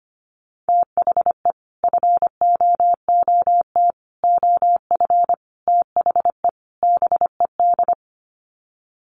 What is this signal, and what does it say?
Morse code, heard as a keyed tone, about 25 wpm, 700 Hz. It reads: THE FOOT OF THE BED
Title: Sets of 5 Words Encore2 25wpm